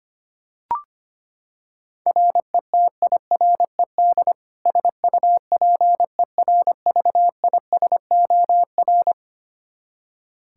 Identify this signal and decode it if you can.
Morse code, heard as a keyed tone, about 25 wpm, 700 Hz. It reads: RETIRED SUPERVISOR
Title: K1USN SST QSO Practice 25wpm